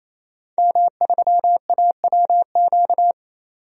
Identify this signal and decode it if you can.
Morse code, heard as a keyed tone, about 28 wpm, 700 Hz. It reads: M3AWQ
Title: DXCC-100 Call Signs - 5 Characters - 28wpm